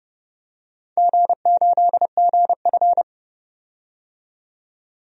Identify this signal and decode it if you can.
Morse code, heard as a keyed tone, about 29 wpm, 700 Hz. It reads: G8GF
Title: DXCC-100 Call Signs - 4 Characters - 30wpm